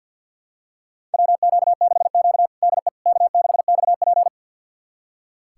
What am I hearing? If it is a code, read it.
Morse code, heard as a keyed tone, about 46 wpm, 700 Hz. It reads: WQ6Q DE K6XP
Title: USA Callsign DE Callsign - 4 Characters - 50wpm